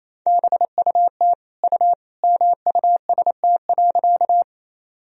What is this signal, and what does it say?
Morse code, heard as a keyed tone, about 27 wpm, 700 Hz. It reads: BUT U MUST.
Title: Sentences from Top 200 Words and Abbreviations 28wpm